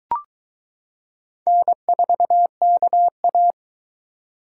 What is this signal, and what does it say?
Morse code, heard as a keyed tone, about 22 wpm, 700 Hz. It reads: N4KA